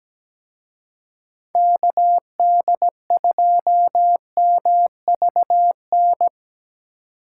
Morse code, heard as a keyed tone, about 20 wpm, 700 Hz. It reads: KD2MVN
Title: Speed Racing - USA 2x3 Call Signs 17wpm